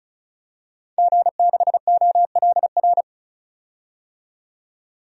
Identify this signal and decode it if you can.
Morse code, heard as a keyed tone, about 36 wpm, 700 Hz. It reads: G6OLR
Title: DXCC-100 Call Signs - 5 Characters - 35wpm